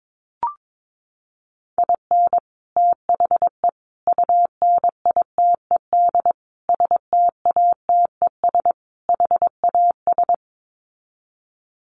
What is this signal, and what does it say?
Morse code, heard as a keyed tone, about 22 wpm, 700 Hz. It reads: IN THE UNITED STATES HAS
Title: Sets of 5 Words Encore2 22wpm